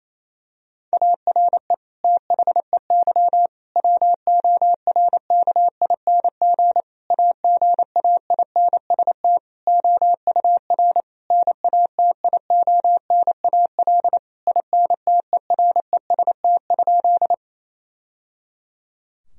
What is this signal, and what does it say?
Morse code, heard as a keyed tone, about 28 wpm, 700 Hz. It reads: ARE THEY WORKING AGAINST OUR NATIONAL INTEREST?
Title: Sentences from Top 1000 Words - Review Part 4 - 28wpm